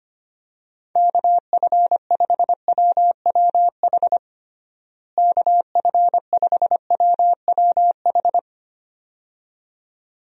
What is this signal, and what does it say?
Morse code, heard as a keyed tone, about 25 wpm, 700 Hz. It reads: KF5WWH KF5WWH
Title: USA Call Signs - 6 Characters - Repeated 2x - 25wpm